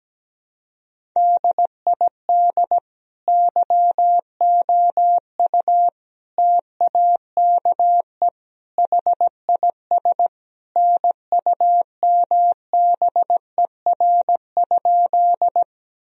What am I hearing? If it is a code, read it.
Morse code, heard as a keyed tone, about 17 wpm, 700 Hz. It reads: DID YOU TAKE HIS NUMBER?
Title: Sentences from Top 1000 Words - Review Part 1 - 17wpm